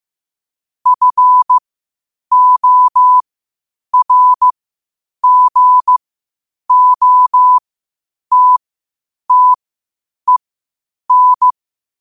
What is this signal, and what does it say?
Morse code, heard as a keyed tone, about 15 wpm, spaced as for 8 wpm, 1000 Hz. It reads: FORGOTTEN
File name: qotdfiles / 8wpm